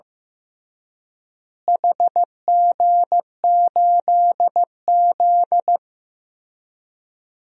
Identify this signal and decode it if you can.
Morse code, heard as a keyed tone, about 15 wpm, 700 Hz. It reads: HG8Z
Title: DXCC-100 Call Signs - 4 Characters - Repeated 3x - 15wpm